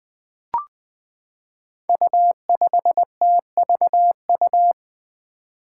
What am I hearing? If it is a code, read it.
Morse code, heard as a keyed tone, about 20 wpm, 700 Hz. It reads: U5TVU
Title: Five Challenging Letter-Number Combinations 20wpm